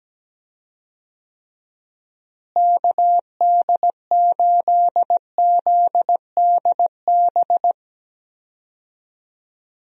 Morse code, heard as a keyed tone, about 17 wpm, 700 Hz. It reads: KD8ZDB
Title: USA Call Signs - 6 Characters - Repeated 3x - 17wpm